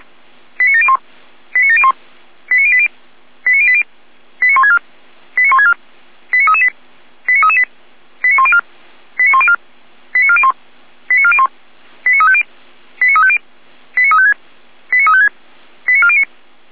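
Five-tone selective call (ZVEI), heard as quick runs of five tones, tones of about 70 ms. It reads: 89701 89701 89080 89080 89164 89164 89338 89338 89104 89104 89401 89401 89350 89350 89356 89356 89409
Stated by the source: ZVEI five-tone selective call, received on an SDR.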